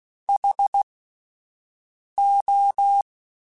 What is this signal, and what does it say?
Morse code, heard as a keyed tone, about 16 wpm, spaced as for 5 wpm, 800 Hz. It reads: HO
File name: Alphabetical / Alphabetical mp3s / Hs @5